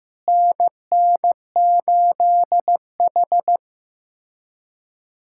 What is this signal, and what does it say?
Morse code, heard as a keyed tone, about 15 wpm, 700 Hz. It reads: NN8H
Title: USA Callsign DE Callsign 15wpm